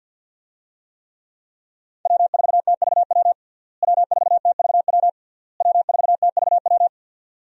Morse code, heard as a keyed tone, about 51 wpm, 700 Hz. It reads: W4TVW W4TVW W4TVW
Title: USA Call Signs - 5 Characters - Repeated 3x - 50wpm